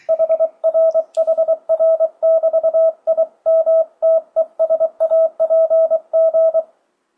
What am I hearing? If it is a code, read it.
Morse code, heard as a keyed tone, about 23 wpm, 650 Hz. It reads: HRHR<BT>IMTESAPG
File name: 2025cw / speedup / 24wpm